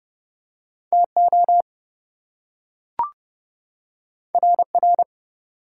Morse code, heard as a keyed tone, about 27 wpm, 700 Hz. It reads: TO RR